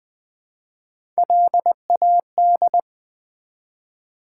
Morse code, heard as a keyed tone, about 20 wpm, 700 Hz. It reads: LAD
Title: Build A Word - 1 Letter At A Time - 4-Letter Words - 20wpm